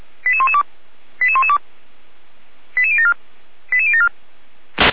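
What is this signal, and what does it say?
Five-tone selective call (ZVEI), heard as quick runs of five tones, tones of about 70 ms. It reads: 88112 88112 88074 88074
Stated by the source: ZVEI five-tone selective call 88519 received on SDR.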